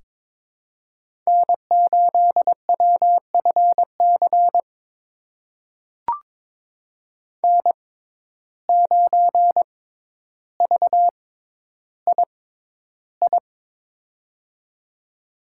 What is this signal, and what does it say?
Morse code, heard as a keyed tone, about 22 wpm, spaced as for 11 wpm, 700 Hz. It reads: N8WFC N9VII
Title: USA Call Signs - 5 Characters - 6x Character Spacing - 22wpm